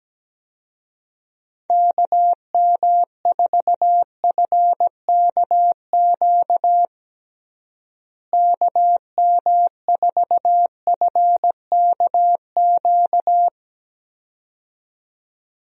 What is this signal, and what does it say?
Morse code, heard as a keyed tone, about 17 wpm, 700 Hz. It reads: KM4FKQ KM4FKQ
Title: USA Call Signs - 6 Characters - Repeated 2x - 17wpm